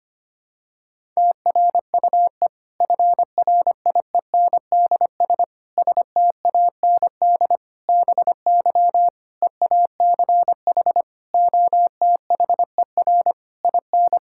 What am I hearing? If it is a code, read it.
Morse code, heard as a keyed tone, about 25 wpm, 700 Hz. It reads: TRUE FRIENDS STAND BY EACH OTHER IN
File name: practice / 25wpm / Difficult Sentences with Hints - 25wpm